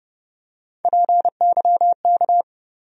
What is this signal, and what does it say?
Morse code, heard as a keyed tone, about 30 wpm, 700 Hz. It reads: PYK
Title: USA Call Sign Postfixes - 3 Letters 30wpm